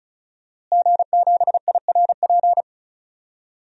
Morse code, heard as a keyed tone, about 36 wpm, 690 Hz. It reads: G7IRP
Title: DXCC-100 Call Signs - 5 Characters - Repeated 3x - 35wpm